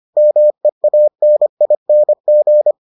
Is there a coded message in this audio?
Yes: Morse code from 0.2 to 2.7 s, heard as a keyed tone, about 25 wpm, 600 Hz. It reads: MEANING